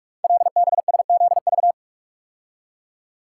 Morse code, heard as a keyed tone, about 44 wpm, 700 Hz. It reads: LBSZV
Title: Code Groups - 5 Letters - 45wpm